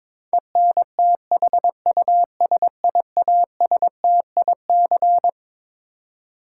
Morse code, heard as a keyed tone, about 22 wpm, 700 Hz. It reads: ENTHUSIASTIC
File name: Twelve Letter Words - 22wpm